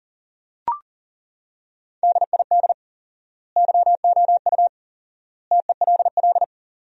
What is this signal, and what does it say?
Morse code, heard as a keyed tone, about 40 wpm, 700 Hz. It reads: DID YOU TELL
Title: 4x Word-Spacing - Sentences from Top 400 Words - 40wpm